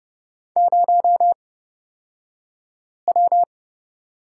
Morse code, heard as a keyed tone, about 30 wpm, spaced as for 5 wpm, 700 Hz. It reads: 0W
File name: Lesson 36d - BT - Two Characters 30wpm